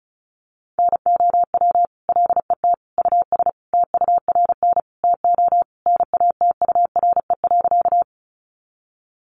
Morse code, heard as a keyed tone, about 34 wpm, 700 Hz. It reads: NOW LET US TURN TO NATURE.
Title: Sentences from Top 600 Words 35wpm